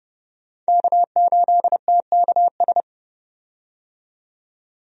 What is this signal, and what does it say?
Morse code, heard as a keyed tone, about 30 wpm, 700 Hz. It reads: K8TKS
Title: USA Call Signs - 5 Characters - 30wpm